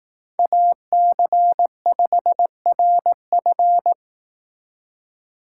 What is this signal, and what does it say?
Morse code, heard as a keyed tone, about 18 wpm, 700 Hz. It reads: AC5RF
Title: Speed Racing - USA 2x2 Call Signs 15wpm